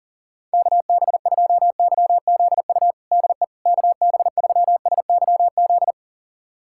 Morse code, heard as a keyed tone, about 40 wpm, 700 Hz. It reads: KB2YZU DE KB3SYZ
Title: USA Callsign DE Callsign 40wpm